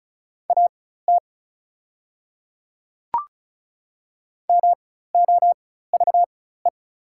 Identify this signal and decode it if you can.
Morse code, heard as a keyed tone, about 36 wpm, spaced as for 16 wpm, 700 Hz. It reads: AT MOVE
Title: 4x Character Spacing - Top 200 Words - 35wpm